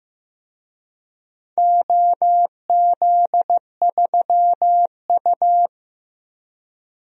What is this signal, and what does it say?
Morse code, heard as a keyed tone, about 15 wpm, 700 Hz. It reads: OZ3U
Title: DXCC-100 Call Signs - 4 Characters - 15wpm